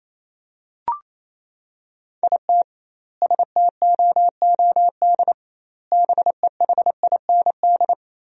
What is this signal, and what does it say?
Morse code, heard as a keyed tone, about 28 wpm, 700 Hz. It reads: IT STOOD BEHIND
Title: 2x Word-Spacing - Sentences from Top 500 Words - 28wpm